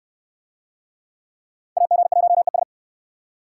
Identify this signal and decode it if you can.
Morse code, heard as a keyed tone, about 64 wpm, 700 Hz. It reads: AG1ES